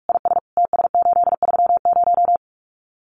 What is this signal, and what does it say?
Morse code, heard as a keyed tone, about 44 wpm, 700 Hz. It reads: IS TS830
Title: Common QSO Phrases 45wpm